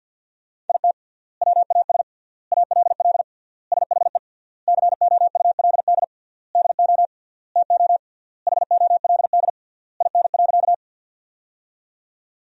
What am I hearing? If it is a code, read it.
Morse code, heard as a keyed tone, about 50 wpm, 700 Hz. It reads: IT WAS ALL SHE COULD DO TO HOLD IN.
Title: 3x Word-Spacing - Sentences from Top 400 Words - 50wpm